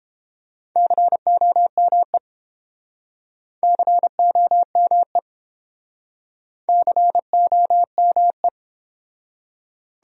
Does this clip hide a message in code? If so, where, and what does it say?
Yes: Morse code from 0.8 to 8.5 s, heard as a keyed tone, about 28 wpm, 700 Hz. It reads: COME COME COME